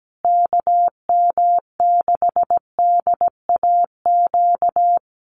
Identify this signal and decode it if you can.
Morse code, heard as a keyed tone, about 17 wpm, 700 Hz. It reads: KM6DAQ
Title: USA Call Signs Repeated 3x 17wpm